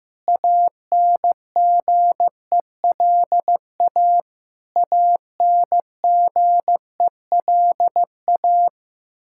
Morse code, heard as a keyed tone, about 15 wpm, 700 Hz. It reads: ANGELA ANGELA
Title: Practice QSO - Exchange 1 - Beginning - 15wpm